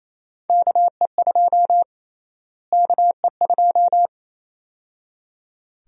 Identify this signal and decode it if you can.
Morse code, heard as a keyed tone, about 28 wpm, 700 Hz. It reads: KE2 KE2
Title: USA Call Sign Prefixes - Repeated 2x - 28wpm